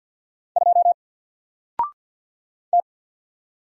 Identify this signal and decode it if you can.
Morse code, heard as a keyed tone, about 48 wpm, 700 Hz. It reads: 2 T
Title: Single Character (Letter:Number:Punctuation:Prosign) Alt 50wpm